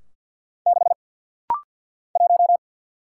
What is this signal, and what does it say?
Morse code, heard as a keyed tone, about 50 wpm, 700 Hz. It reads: 6 1
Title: Single Letter-Number - Rapid-Fire 50wpm